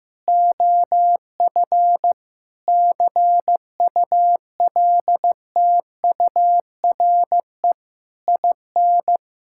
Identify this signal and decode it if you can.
Morse code, heard as a keyed tone, about 15 wpm, 700 Hz. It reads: OF CULTURE IN
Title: Sentences from Top 1000 Words - Review Part 6 - 15wpm